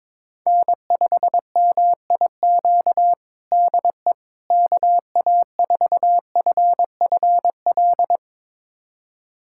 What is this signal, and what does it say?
Morse code, heard as a keyed tone, about 22 wpm, 700 Hz. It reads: N5MIQ DE KA4FFL
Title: USA Callsign DE Callsign 22wpm